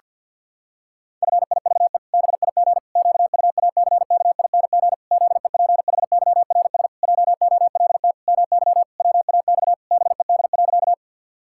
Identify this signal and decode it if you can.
Morse code, heard as a keyed tone, about 52 wpm, 700 Hz. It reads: FIVE BIG QUACKING ZEPHYRS JOLT MY WAX BED.
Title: Pangram Challenge v2 - 50wpm